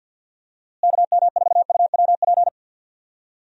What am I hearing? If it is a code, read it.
Morse code, heard as a keyed tone, about 52 wpm, 700 Hz. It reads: KM4UWP
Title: USA Call Signs - 6 Characters - Repeated 3x - 50wpm